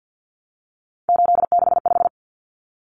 Morse code, heard as a keyed tone, about 50 wpm, 700 Hz. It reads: 865
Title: Common Cut Numbers - 3-Digit 50wpm